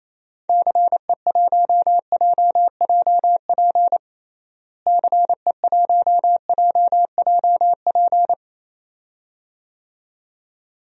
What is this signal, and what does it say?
Morse code, heard as a keyed tone, about 28 wpm, 700 Hz. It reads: CE1JJP CE1JJP
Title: DXCC-100 Call Signs - 6 Characters - Repeated 2x - 28wpm